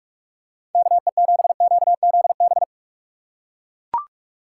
Morse code, heard as a keyed tone, about 43 wpm, 690 Hz. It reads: KE7QZB
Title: USA Call Signs - 6 Characters - 3x Character Spacing - 45wpm